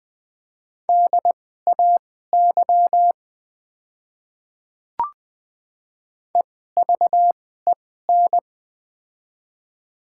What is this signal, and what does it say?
Morse code, heard as a keyed tone, about 20 wpm, spaced as for 14 wpm, 700 Hz. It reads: DAY EVEN
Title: 2x Character Spacing - Top 200 Words - 20wpm